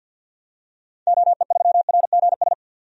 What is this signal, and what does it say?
Morse code, heard as a keyed tone, about 52 wpm, 700 Hz. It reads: OE3RGS
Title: DXCC-100 Call Signs - 6 Characters - 50wpm